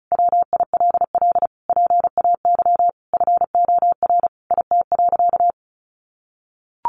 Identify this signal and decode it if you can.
Morse code, heard as a keyed tone, about 34 wpm, 700 Hz. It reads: WILL PAY FOR IT.
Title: Sentences from Top 1300 Words 35wpm